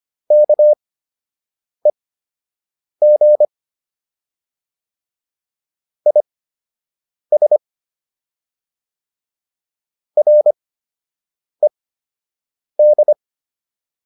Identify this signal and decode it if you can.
Morse code, heard as a keyed tone, about 25 wpm, spaced as for 7 wpm, 600 Hz. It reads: KEG IS RED